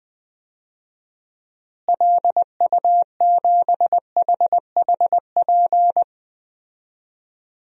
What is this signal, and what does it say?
Morse code, heard as a keyed tone, about 21 wpm, 700 Hz. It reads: LU7HHP
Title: DXCC-100 Call Signs - 6 Characters - Repeated 2x - 20wpm